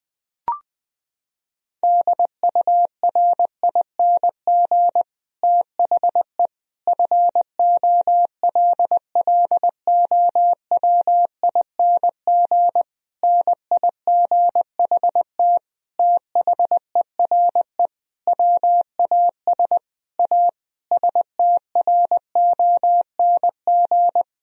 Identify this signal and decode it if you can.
Morse code, heard as a keyed tone, about 19 wpm, 700 Hz. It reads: DURING THE FOLLOWING NIGHT THERE WAS A STRONG
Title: Sentences from Top 1000 Words - Review Part 4 - 20wpm